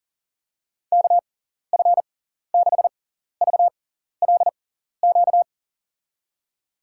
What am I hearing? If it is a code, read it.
Morse code, heard as a keyed tone, about 40 wpm, spaced as for 14 wpm, 700 Hz. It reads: KF6VLQ
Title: USA Call Signs - 6 Characters - 6x Character Spacing - 40wpm